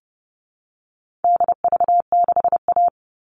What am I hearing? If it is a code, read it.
Morse code, heard as a keyed tone, about 30 wpm, 700 Hz. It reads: DV6A
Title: All Cut Numbers - 4-Digit 30wpm